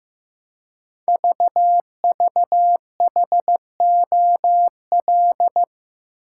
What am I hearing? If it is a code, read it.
Morse code, heard as a keyed tone, about 15 wpm, 700 Hz. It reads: VVHOL
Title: Code Groups - 5 Letters - 15wpm